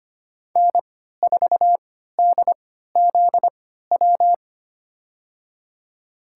Morse code, heard as a keyed tone, about 25 wpm, spaced as for 14 wpm, 700 Hz. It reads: N4DZW
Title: USA Call Signs - 3x Character Spacing - 25wpm